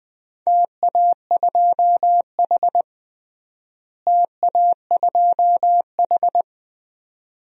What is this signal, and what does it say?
Morse code, heard as a keyed tone, about 20 wpm, 700 Hz. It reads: TA2H TA2H
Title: DXCC-100 Call Signs - 4 Characters - Repeated 2x - 20wpm